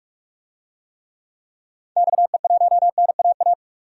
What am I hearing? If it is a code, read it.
Morse code, heard as a keyed tone, about 44 wpm, 700 Hz. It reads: XE1NAA
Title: DXCC-100 Call Signs - 6 Characters - 45wpm